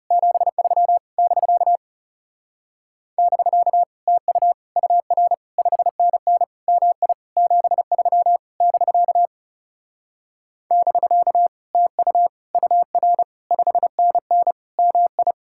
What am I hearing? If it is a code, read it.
Morse code, heard as a keyed tone, about 37 wpm, 700 Hz. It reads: 73 <BK> <BK> TU UR 5NN MI 73 <BK> <BK> TU UR 5NN MI
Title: Speed-Racing Common QSO Phrases Encore - 30wpm